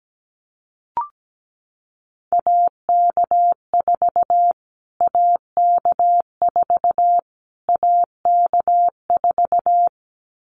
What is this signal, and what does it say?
Morse code, heard as a keyed tone, about 17 wpm, 700 Hz. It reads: AK4 AK4 AK4
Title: Callsign Prefixes for Top 25 DXCC - Repeated 3x - 17wpm